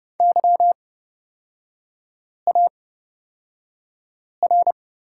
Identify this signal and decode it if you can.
Morse code, heard as a keyed tone, about 31 wpm, spaced as for 5 wpm, 690 Hz. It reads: YAR
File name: Lesson 38e - Letter q - Three Characters 30wpm